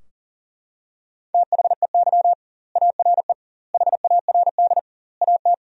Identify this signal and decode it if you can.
Morse code, heard as a keyed tone, about 40 wpm, 700 Hz. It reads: THEY ARE HARD AT
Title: 2x Word-Spacing - Sentences from Top 300 Words - 40wpm